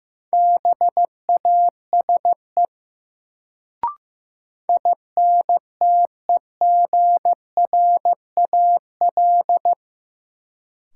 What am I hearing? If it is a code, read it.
Morse code, heard as a keyed tone, about 15 wpm, 700 Hz. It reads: BASE INTEGRAL
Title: Sets of 5 Words - New Word Drill - 15wpm